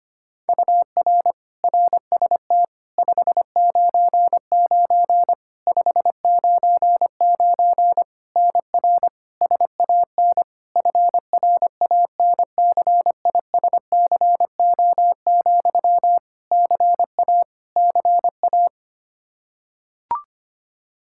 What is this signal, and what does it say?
Morse code, heard as a keyed tone, about 25 wpm, 700 Hz. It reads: UR RST 599 599 NR SAN FRANCISCO, CA CA
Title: Practice QSO - Exchange 1 - Beginning - 25wpm